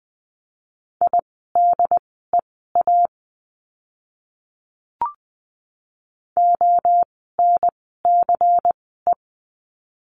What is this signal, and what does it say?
Morse code, heard as a keyed tone, about 20 wpm, spaced as for 15 wpm, 700 Hz. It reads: IDEA ONCE
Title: 108 Lesson 18f - Letter m - Words 20wpm